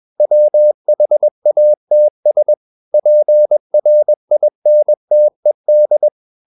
Morse code, heard as a keyed tone, about 21 wpm, 600 Hz. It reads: WHATS PRINTED
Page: CW med Gnister